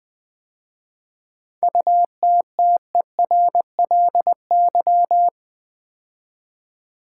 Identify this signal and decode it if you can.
Morse code, heard as a keyed tone, about 20 wpm, 700 Hz. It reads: UTTERLY
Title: 7 Letter Words - First 3 Letters Given - 20wpm